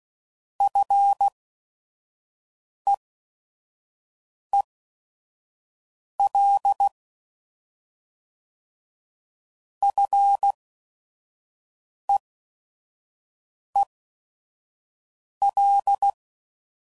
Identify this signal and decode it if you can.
Morse code, heard as a keyed tone, about 16 wpm, spaced as for 5 wpm, 800 Hz. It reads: FEEL FEEL